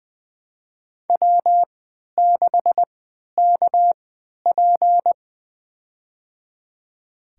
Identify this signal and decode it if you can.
Morse code, heard as a keyed tone, about 20 wpm, spaced as for 11 wpm, 700 Hz. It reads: W6KP
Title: USA Call Signs - 4 Characters - 3x Character Spacing - 20wpm